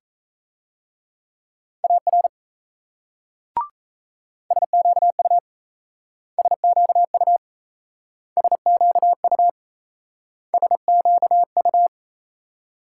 Morse code, heard as a keyed tone, about 35 wpm, 700 Hz. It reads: AR SQU SQU SQU SQU